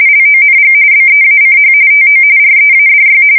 RTTY, about 45.45 baud. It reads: LONG BEACH TO READY
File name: fsk170 / battleshipnj